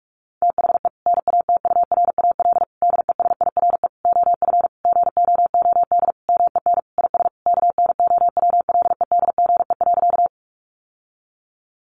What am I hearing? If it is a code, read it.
Morse code, heard as a keyed tone, about 44 wpm, 700 Hz. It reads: THE NATURAL DESIRE OF GOOD MEN IS KNOWLEDGE.
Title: Sentences from Top 1100 Words 45wpm